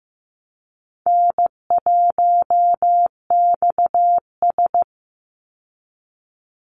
Morse code, heard as a keyed tone, about 15 wpm, 700 Hz. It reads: N1XS
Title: Easy Call Signs Repeated 3x - 15wpm